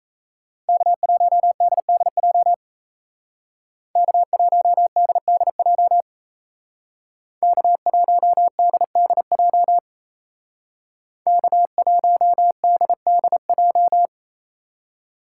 Morse code, heard as a keyed tone, about 35 wpm, 700 Hz. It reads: K1DDJ K1DDJ K1DDJ K1DDJ